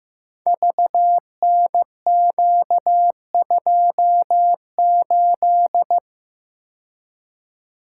Morse code, heard as a keyed tone, about 15 wpm, 700 Hz. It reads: VNQ28
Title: Code Groups - 5 Letter:Numbers - 15wpm